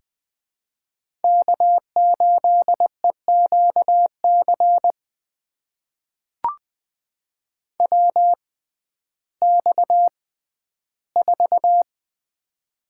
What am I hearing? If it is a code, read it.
Morse code, heard as a keyed tone, about 20 wpm, spaced as for 12 wpm, 700 Hz. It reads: K8EQC WX4
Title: USA Call Signs - 5 Characters - 6x Character Spacing - 20wpm